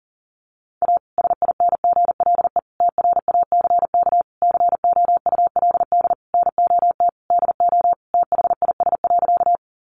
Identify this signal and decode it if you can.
Morse code, heard as a keyed tone, about 40 wpm, 700 Hz. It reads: A SINGLE TRACK COULD NOT DO THIS.
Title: Sentences from Top 1100 Words 40wpm